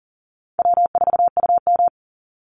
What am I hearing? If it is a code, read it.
Morse code, heard as a keyed tone, about 40 wpm, 700 Hz. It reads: W4UM
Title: Easy Call Signs Repeated 2x - 40wpm